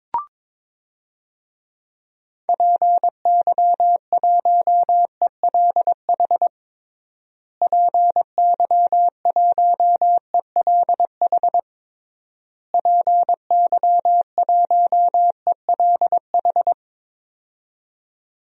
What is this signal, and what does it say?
Morse code, heard as a keyed tone, about 22 wpm, 700 Hz. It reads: PY1ELH PY1ELH PY1ELH
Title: DXCC-100 Call Signs - 6 Characters - Repeated 3x - 22wpm